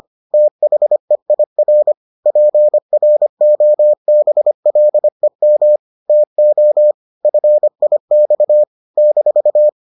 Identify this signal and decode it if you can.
Morse code, heard as a keyed tone, about 25 wpm, 600 Hz. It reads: THEIR PROBLEM TO FIX -